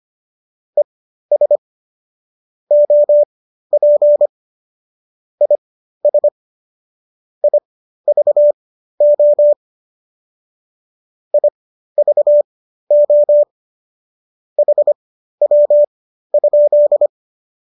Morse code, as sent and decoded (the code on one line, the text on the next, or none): . ... / --- .--. / .. ... / .. ...- --- / .. ...- --- / .... .-- ..--..
ES OP IS IVO IVO HW?